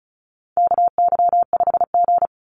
Morse code, heard as a keyed tone, about 34 wpm, 700 Hz. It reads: KY5G
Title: USA Call Signs 35wpm